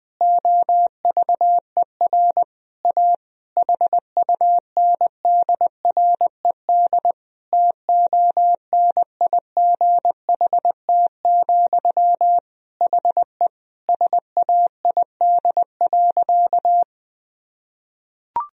Morse code, heard as a keyed tone, about 20 wpm, 700 Hz. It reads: OVER A HUNDRED TONIGHT, HE SAID.
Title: Sentences from Top 1000 Words - Review Part 2 - 20wpm